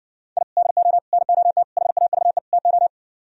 Morse code, heard as a keyed tone, about 69 wpm, 700 Hz. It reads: I DO NOT HAVE TO